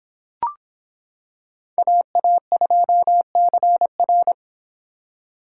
Morse code, heard as a keyed tone, about 26 wpm, 700 Hz. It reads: AA2CR